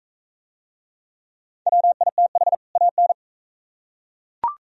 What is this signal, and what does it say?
Morse code, heard as a keyed tone, about 41 wpm, 700 Hz. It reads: WITH AN